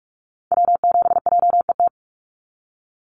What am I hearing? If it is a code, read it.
Morse code, heard as a keyed tone, about 44 wpm, 700 Hz. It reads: W7JET
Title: NA SOTA Activator Callsigns 45wpm